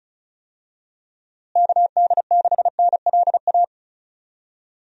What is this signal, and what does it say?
Morse code, heard as a keyed tone, about 36 wpm, 700 Hz. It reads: KD6NLA
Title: USA Call Signs - 6 Characters - Repeated 2x - 35wpm